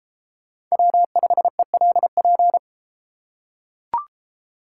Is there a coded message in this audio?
Yes: Morse code from 0.7 to 2.6 s, heard as a keyed tone, about 32 wpm, 700 Hz. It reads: W5ELP